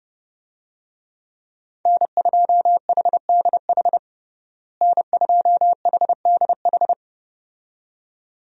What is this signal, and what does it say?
Morse code, heard as a keyed tone, about 30 wpm, 700 Hz. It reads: N2HDH N2HDH
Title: USA Call Signs - 5 Characters - Repeated 2x - 30wpm